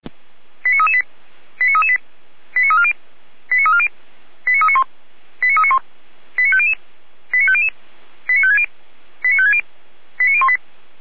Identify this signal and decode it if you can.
Five-tone selective call (ZVEI), heard as quick runs of five tones, tones of about 70 ms. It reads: 89338 89338 89340 89340 89391 89391 89500 89500 89560 89560 89018
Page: SDR/ZVEI